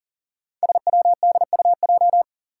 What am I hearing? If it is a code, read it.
Morse code, heard as a keyed tone, about 40 wpm, 700 Hz. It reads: SWDUJ
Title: Five Challenging Letter-Number Combinations 40wpm